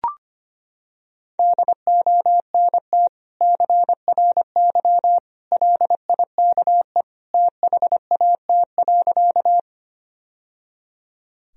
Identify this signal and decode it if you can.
Morse code, heard as a keyed tone, about 25 wpm, 700 Hz. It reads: DONT CRY LIKE THAT.
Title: Sentences from Top 1000 Words - Review Part 2 - 25wpm